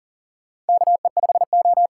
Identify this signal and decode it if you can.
Morse code, heard as a keyed tone, about 38 wpm, 700 Hz. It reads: KE5O
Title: USA Call Signs - 8x Character Spacing - 40wpm